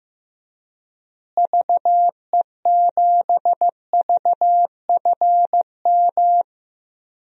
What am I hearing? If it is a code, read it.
Morse code, heard as a keyed tone, about 15 wpm, 700 Hz. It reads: VE7VFM
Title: DXCC-100 Call Signs - 6 Characters - Repeated 3x - 15wpm